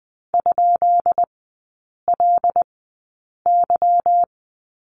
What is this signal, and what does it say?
Morse code, heard as a keyed tone, about 20 wpm, spaced as for 8 wpm, 700 Hz. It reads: ?LY
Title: 198 Lesson 32e - Letter k - Three Characters 20wpm